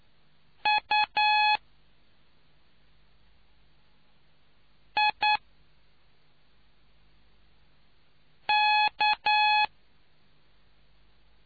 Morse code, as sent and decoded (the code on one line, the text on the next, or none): ..- / .. / -.-
U I K